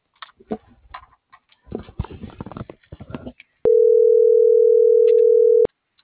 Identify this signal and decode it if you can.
Ringback tone (North America), 440 + 480 Hz.